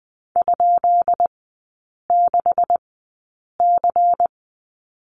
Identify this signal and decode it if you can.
Morse code, heard as a keyed tone, about 20 wpm, spaced as for 8 wpm, 700 Hz. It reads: ?6C
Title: 152 Lesson 25e - Letter p - Three Characters 20wpm